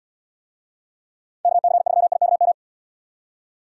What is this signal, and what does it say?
Morse code, heard as a keyed tone, about 72 wpm, 700 Hz. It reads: KC3IKM